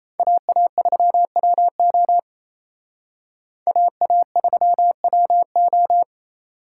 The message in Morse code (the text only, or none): AA3WO AA3WO